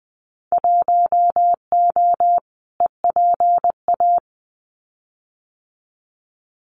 Morse code, heard as a keyed tone, about 20 wpm, 700 Hz. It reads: 1O EPA
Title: Winter Field Day Exchanges 20wpm